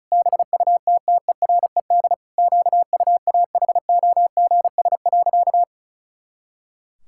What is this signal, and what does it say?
Morse code, heard as a keyed tone, about 36 wpm, 700 Hz. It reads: BUTTERED QUAHOGS.
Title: Pangram Challenge v2 - 35wpm